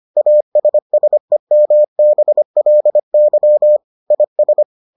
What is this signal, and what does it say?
Morse code, heard as a keyed tone, about 25 wpm, 600 Hz. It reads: ASSEMBLY IS